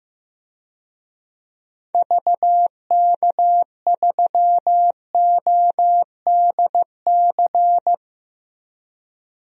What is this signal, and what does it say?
Morse code, heard as a keyed tone, about 15 wpm, 700 Hz. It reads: VK3ODC
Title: DXCC-100 Call Signs - 6 Characters - 15wpm